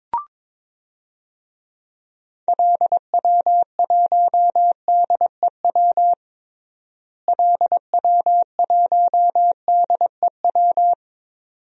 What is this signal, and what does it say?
Morse code, heard as a keyed tone, about 22 wpm, 700 Hz. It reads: LW1DEW LW1DEW
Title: DXCC-100 Call Signs - 6 Characters - Repeated 2x - 22wpm